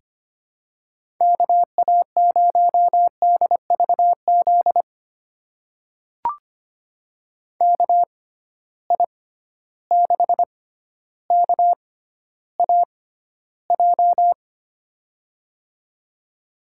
Morse code, heard as a keyed tone, about 25 wpm, spaced as for 13 wpm, 700 Hz. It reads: KA0DVZ KI6KAJ
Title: USA Call Signs - 6 Characters - 6x Character Spacing - 25wpm